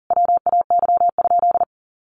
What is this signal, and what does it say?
Morse code, heard as a keyed tone, about 37 wpm, 700 Hz. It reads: WAY?
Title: Sentences from Top 200 Words 40wpm